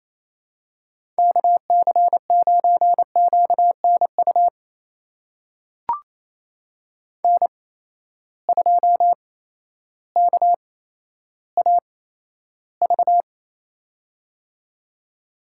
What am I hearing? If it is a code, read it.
Morse code, heard as a keyed tone, about 28 wpm, spaced as for 13 wpm, 700 Hz. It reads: KC9QNU N2KAV
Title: USA Call Signs - 8x Character Spacing - 28wpm